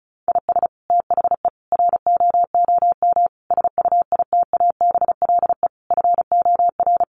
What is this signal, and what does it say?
Morse code, heard as a keyed tone, about 34 wpm, 700 Hz. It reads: IS THE ROOM SUITABLE FOR
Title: Sentences from Top 1400 Words 35wpm